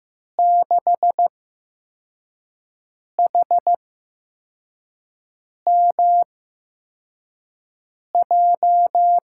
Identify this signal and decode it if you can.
Morse code, heard as a keyed tone, about 15 wpm, spaced as for 4 wpm, 700 Hz. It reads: 6HMJ
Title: USA Call Signs - 6 Characters - 8x Character Spacing - 15wpm